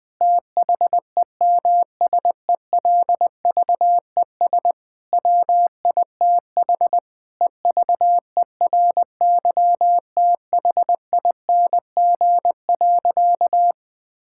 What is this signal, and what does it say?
Morse code, heard as a keyed tone, about 20 wpm, 700 Hz. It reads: THEMSELVES WITH EVERYTHING.
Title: Sentences from Top 1000 Words - Review Part 5 - 20wpm